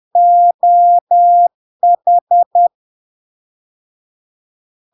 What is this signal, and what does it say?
Morse code, heard as a keyed tone, about 10 wpm, 700 Hz. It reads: OH
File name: LARK-FD-2022-Log-in-CW-10wpm-700-Hz